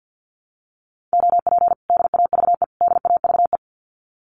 Morse code, heard as a keyed tone, about 50 wpm, 700 Hz. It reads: OP DAVE DAVE
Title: Common QSO Phrases Encore 50wpm